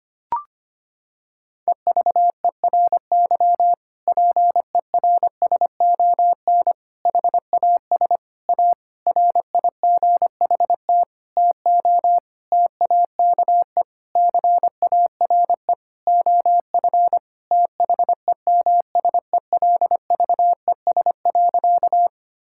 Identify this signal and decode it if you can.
Morse code, heard as a keyed tone, about 25 wpm, 700 Hz. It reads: EVERY PERSON HAS A RIGHT TO TAKE CARE OF THEMSELVES.
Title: Sentences from Top 1000 Words - Review Part 5 - 25wpm